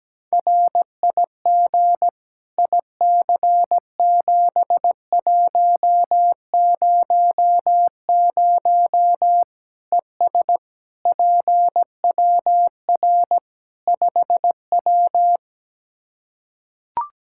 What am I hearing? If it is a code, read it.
Morse code, heard as a keyed tone, about 17 wpm, 700 Hz. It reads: RIG IC7100 ES PWR 5W
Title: Practice QSO - Exchange 2 - Info - 17wpm